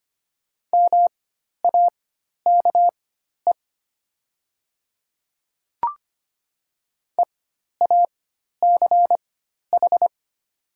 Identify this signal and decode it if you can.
Morse code, heard as a keyed tone, about 25 wpm, spaced as for 12 wpm, 700 Hz. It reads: MAKE EACH
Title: 4x Character Spacing - Top 100 Words - 25wpm